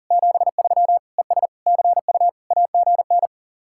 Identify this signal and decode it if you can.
Morse code, heard as a keyed tone, about 52 wpm, 700 Hz. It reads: 73 ES CU AGN